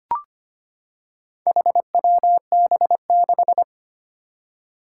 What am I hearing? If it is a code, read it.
Morse code, heard as a keyed tone, about 25 wpm, 700 Hz. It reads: HWB6
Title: Four Challenging Letter-Number Combinations 25wpm